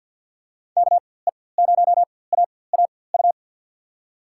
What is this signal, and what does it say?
Morse code, heard as a keyed tone, about 48 wpm, spaced as for 23 wpm, 700 Hz. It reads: KE0AAU
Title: USA Call Signs - 6 Characters - 4x Character Spacing - 50wpm